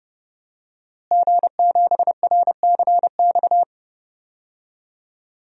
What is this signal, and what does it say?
Morse code, heard as a keyed tone, about 30 wpm, 700 Hz. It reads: G7RCX
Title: DXCC-100 Call Signs - 5 Characters - Repeated 3x - 30wpm